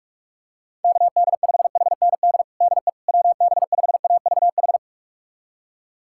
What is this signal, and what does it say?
Morse code, heard as a keyed tone, about 46 wpm, 700 Hz. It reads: KD5HND DE WB5AVH
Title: USA Callsign DE Callsign - 6 Characters - 45wpm